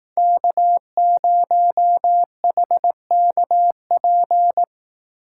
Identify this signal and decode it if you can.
Morse code, heard as a keyed tone, about 18 wpm, 700 Hz. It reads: K0HKP
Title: Speed Racing - USA Call Signs 15wpm